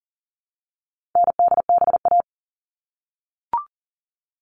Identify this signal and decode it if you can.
Morse code, heard as a keyed tone, about 40 wpm, 700 Hz. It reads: NDBA
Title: All Cut Numbers - 4-Digit 40wpm